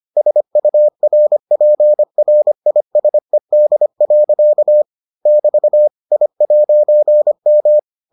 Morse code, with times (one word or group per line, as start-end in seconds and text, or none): 0.2-4.8 SURPRISED.
5.3-5.9 <BT>
6.1-7.8 I'M